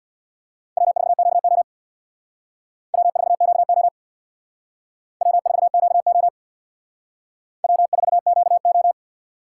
Morse code, heard as a keyed tone, about 64 wpm, 700 Hz. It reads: W4QO W4QO W4QO W4QO